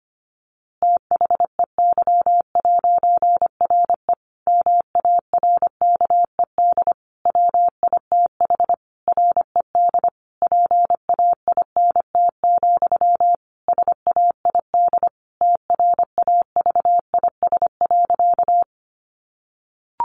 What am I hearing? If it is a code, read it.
Morse code, heard as a keyed tone, about 25 wpm, 700 Hz. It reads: THEY'RE MARKED WITH RED PAINT, SAID TRAVIS.
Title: A Sound Of Thunder 25wpm